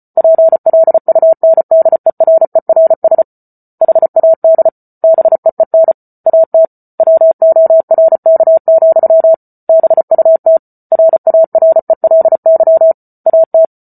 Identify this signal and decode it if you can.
Morse code, heard as a keyed tone, about 34 wpm, 650 Hz. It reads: PLUNDERERS HAD BEEN AT WORK, BUT RARELY AT